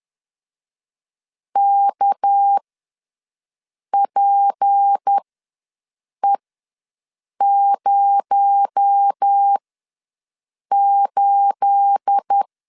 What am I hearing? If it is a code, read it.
Morse code, heard as a keyed tone, about 10 wpm, spaced as for 6 wpm, 800 Hz. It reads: KPE08
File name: LEZIONI / 9 28 ws 10wpm 4124